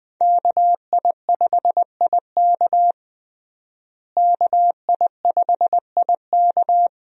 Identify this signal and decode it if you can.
Morse code, heard as a keyed tone, about 20 wpm, 700 Hz. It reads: KI5IK KI5IK
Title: USA Call Signs - 5 Characters - Repeated 2x - 20wpm